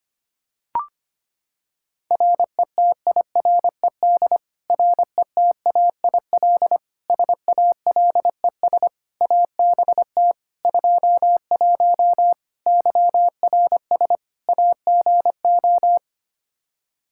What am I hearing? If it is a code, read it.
Morse code, heard as a keyed tone, about 25 wpm, 700 Hz. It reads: RETIRED RETAIL SALES ABT 21 YRS AGO
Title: Practice QSO - Exchange 3 - More Info - 25wpm